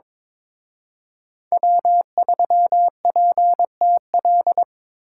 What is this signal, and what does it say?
Morse code, heard as a keyed tone, about 26 wpm, 700 Hz. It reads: W3PTL
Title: Speed Racing - USA 1x3 Call Signs 22wpm